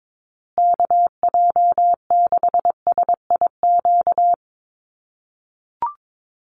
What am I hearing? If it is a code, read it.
Morse code, heard as a keyed tone, about 22 wpm, 700 Hz. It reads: KJ6SIQ
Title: USA Call Signs Repeated 2x 22wpm